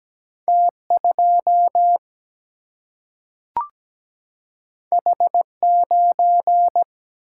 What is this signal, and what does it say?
Morse code, heard as a keyed tone, about 17 wpm, 700 Hz. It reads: T2 H9
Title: Single-Letter Single-Number 17wpm